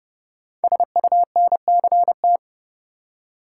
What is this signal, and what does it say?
Morse code, heard as a keyed tone, about 30 wpm, 700 Hz. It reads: SUNCT
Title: Code Groups - 5 Letter:Numbers - 30wpm